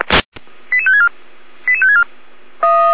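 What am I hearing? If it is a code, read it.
Five-tone selective call (ZVEI), heard as quick runs of five tones, tones of about 80 ms. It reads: 88564 88564
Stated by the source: ZVEI five-tone call received by SDR, feuerwehr-orlamuende.org (fire brigade Orlamünde).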